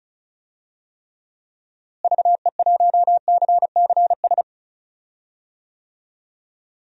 Morse code, heard as a keyed tone, about 36 wpm, 700 Hz. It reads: VE1CCS
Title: DXCC-100 Call Signs - 6 Characters - Repeated 2x - 35wpm